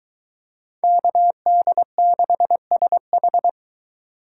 Morse code, heard as a keyed tone, about 23 wpm, 700 Hz. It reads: KD6SH